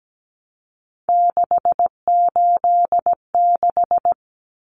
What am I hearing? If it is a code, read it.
Morse code, heard as a keyed tone, about 17 wpm, 690 Hz. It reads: 686
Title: Common Cut Numbers - 3-Digit 17wpm